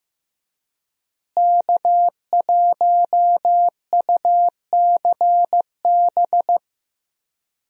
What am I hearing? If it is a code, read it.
Morse code, heard as a keyed tone, about 15 wpm, 700 Hz. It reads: K1UCB
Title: USA Call Signs - 5 Characters - Repeated 2x - 15wpm